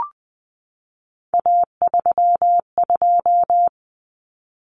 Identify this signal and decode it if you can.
Morse code, heard as a keyed tone, about 20 wpm, 700 Hz. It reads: A32
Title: Common Cut Numbers - 3-Digit 20wpm